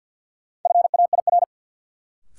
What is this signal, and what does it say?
Morse code, heard as a keyed tone, about 52 wpm, 700 Hz. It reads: FAIR
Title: Sentences from Top 1000 Words - Review Part 5 - New Word Drill - 50wpm